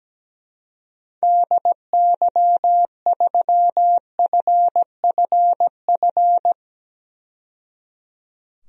Morse code, heard as a keyed tone, about 17 wpm, 700 Hz. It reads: DY3FFF
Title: DXCC-100 Call Signs - 6 Characters - 17wpm